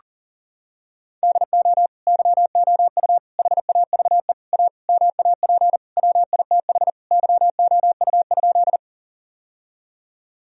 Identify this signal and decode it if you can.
Morse code, heard as a keyed tone, about 40 wpm, 700 Hz. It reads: DO YOU HAVE A MAP WITH YOU?
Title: Sentences from Top 1000 Words - Review Part 2 - 40wpm